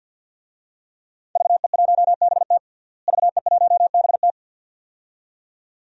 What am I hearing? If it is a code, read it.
Morse code, heard as a keyed tone, about 47 wpm, 700 Hz. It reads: VE1BT VE1BT
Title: DXCC-100 Call Signs - 5 Characters - Repeated 2x - 50wpm